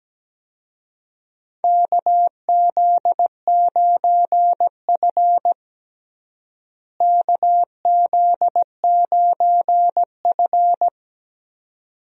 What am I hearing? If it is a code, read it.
Morse code, heard as a keyed tone, about 17 wpm, 700 Hz. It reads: KZ9F KZ9F
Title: USA Call Signs - 4 Characters - Repeated 2x - 17wpm